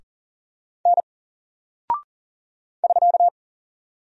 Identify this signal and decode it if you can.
Morse code, heard as a keyed tone, about 40 wpm, 700 Hz. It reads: N <SK>
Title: Single Character (Letter:Number:Punctuation:Prosign) 40wpm